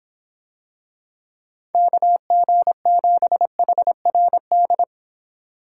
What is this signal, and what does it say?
Morse code, heard as a keyed tone, about 23 wpm, 700 Hz. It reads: KG7HRD